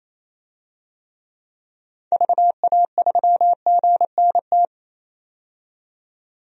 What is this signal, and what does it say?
Morse code, heard as a keyed tone, about 28 wpm, 700 Hz. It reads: VA3GNT
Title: DXCC-100 Call Signs - 6 Characters - 28wpm